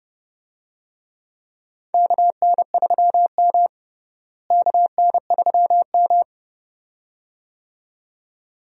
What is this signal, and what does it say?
Morse code, heard as a keyed tone, about 30 wpm, 700 Hz. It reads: KN3M KN3M
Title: USA Call Signs - 4 Characters - Repeated 2x - 30wpm